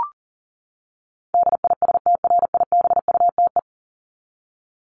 Morse code, heard as a keyed tone, about 40 wpm, 700 Hz. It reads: DISTRIBUTE
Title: Top 1501-1750 Words 40wpm